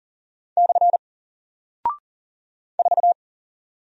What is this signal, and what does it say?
Morse code, heard as a keyed tone, about 40 wpm, 700 Hz. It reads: / 4
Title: Single Character (Letter:Number:Punctuation:Prosign) Alt 40wpm